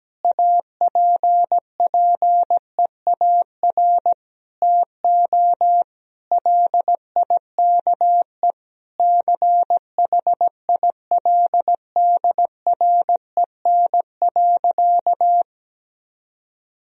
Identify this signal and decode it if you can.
Morse code, heard as a keyed tone, about 17 wpm, 700 Hz. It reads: APPEAR TO LIKE CHILDREN.
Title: Sentences from Top 1000 Words - Review Part 2 - 17wpm